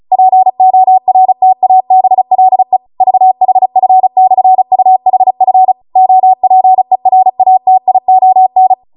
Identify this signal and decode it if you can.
Morse code, heard as a keyed tone, about 36 wpm, 750 Hz. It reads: PORTABLE VHF/UHF OPERATION